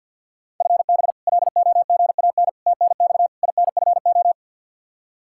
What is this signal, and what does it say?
Morse code, heard as a keyed tone, about 50 wpm, 700 Hz. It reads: FB LOGAN TNX INFO
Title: K1USN SST QSO Practice 50wpm